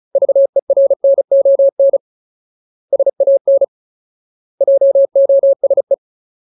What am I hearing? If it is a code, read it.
Morse code, heard as a keyed tone, about 34 wpm, 550 Hz. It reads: VERNON SAN JOSE
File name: Ultimate Word List / 35WPM Ultimate